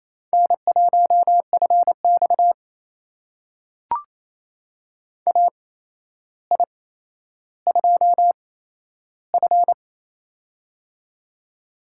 Morse code, heard as a keyed tone, about 28 wpm, spaced as for 12 wpm, 700 Hz. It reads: N1FX AI2F
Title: USA Call Signs - 4 Characters - 8x Character Spacing - 28wpm